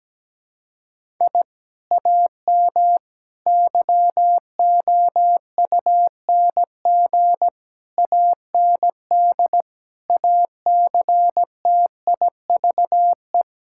Morse code, heard as a keyed tone, about 17 wpm, 700 Hz. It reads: I AM YOUNG AND ACTIVE
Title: Sentences from Top 1000 Words - Review Part 6 - 17wpm